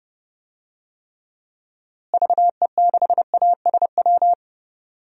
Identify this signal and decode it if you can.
Morse code, heard as a keyed tone, about 30 wpm, 700 Hz. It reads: VE6ASW
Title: DXCC-100 Call Signs - 6 Characters - Repeated 2x - 30wpm